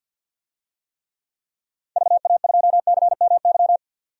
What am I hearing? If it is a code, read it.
Morse code, heard as a keyed tone, about 51 wpm, 700 Hz. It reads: VA2CMY
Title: DXCC-100 Call Signs - 6 Characters - 50wpm